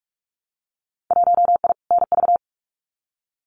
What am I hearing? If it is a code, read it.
Morse code, heard as a keyed tone, about 46 wpm, 700 Hz. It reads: 1I NV
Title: Winter Field Day Exchanges Encore 45wpm